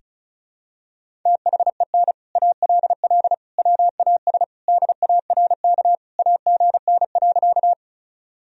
Morse code, heard as a keyed tone, about 36 wpm, 700 Hz. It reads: THEN ALL WAS DARK AGN.
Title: Sentences from Top 500 Words and Abbreviations 35wpm